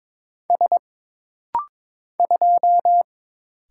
Morse code, heard as a keyed tone, about 22 wpm, 700 Hz. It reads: S 2
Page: This rapid-fire timing is geared to push your ICR (Instant Character Recognition) skills.